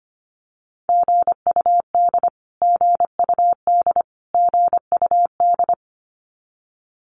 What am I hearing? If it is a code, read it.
Morse code, heard as a keyed tone, about 25 wpm, 700 Hz. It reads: GUD GUD GUD
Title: Abbreviations Repeated 3x - 25wpm